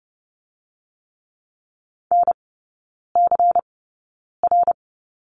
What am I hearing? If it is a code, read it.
Morse code, heard as a keyed tone, about 30 wpm, spaced as for 9 wpm, 690 Hz. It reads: NCR
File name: Lesson 17e - Letter u - Three Characters 30wpm